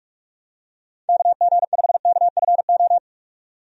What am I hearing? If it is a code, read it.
Morse code, heard as a keyed tone, about 46 wpm, 700 Hz. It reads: KG5KFO
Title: USA Call Signs - 6 Characters - 45wpm